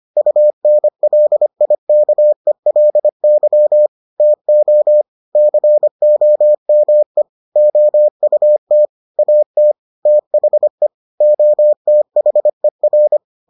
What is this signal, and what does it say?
Morse code, heard as a keyed tone, about 25 wpm, 600 Hz. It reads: UNLIKELY TO COME OUT AT THE OTHER